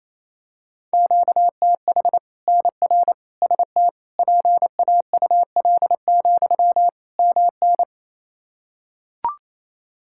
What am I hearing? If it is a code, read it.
Morse code, heard as a keyed tone, about 27 wpm, 700 Hz. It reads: QTH NR ST PAUL, MN
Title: K1USN SST QSO Practice 28wpm